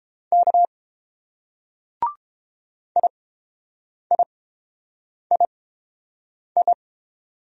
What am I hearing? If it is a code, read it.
Morse code, heard as a keyed tone, about 30 wpm, spaced as for 7 wpm, 700 Hz. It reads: K IIII